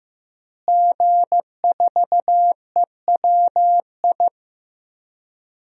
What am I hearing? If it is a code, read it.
Morse code, heard as a keyed tone, about 15 wpm, 700 Hz. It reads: G4EWI
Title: DXCC-100 Call Signs - 5 Characters - Repeated 2x - 15wpm